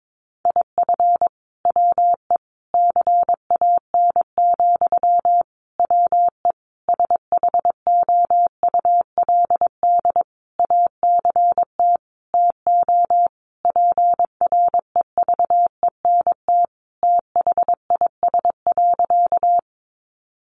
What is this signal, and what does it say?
Morse code, heard as a keyed tone, about 22 wpm, 700 Hz. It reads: IF WE CAN, WE SHOULD ACT TO PREVENT THIS.
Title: Sentences from Top 1100 Words 22wpm